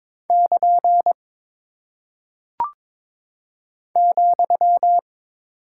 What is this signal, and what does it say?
Morse code, heard as a keyed tone, about 22 wpm, 700 Hz. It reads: <KN> ,
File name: practice / 22wpm / Single Punctuation-Prosign - 22wpm